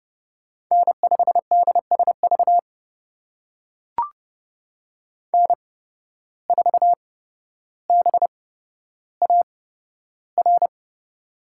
Morse code, heard as a keyed tone, about 30 wpm, spaced as for 13 wpm, 700 Hz. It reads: N5DSV N4BAR
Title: USA Call Signs - 5 Characters - 8x Character Spacing - 30wpm